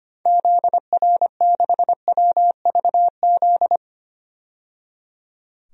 Morse code, heard as a keyed tone, about 25 wpm, 700 Hz. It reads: ZR6WVZ
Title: DXCC-100 Call Signs - 6 Characters - 25wpm